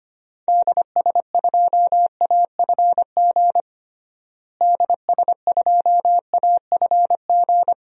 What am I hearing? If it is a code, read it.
Morse code, heard as a keyed tone, about 25 wpm, 700 Hz. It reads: DS2AFG DS2AFG
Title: DXCC-100 Call Signs - 6 Characters - Repeated 2x - 25wpm